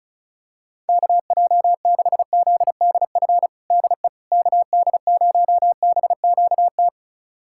Morse code, heard as a keyed tone, about 34 wpm, 700 Hz. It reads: KJ6ZDF DE KD0BQT
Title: USA Callsign DE Callsign - 6 Characters - 35wpm